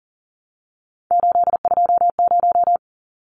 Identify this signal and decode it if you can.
Morse code, heard as a keyed tone, about 40 wpm, 700 Hz. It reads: 820
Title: Numbers - 3-Digit 40wpm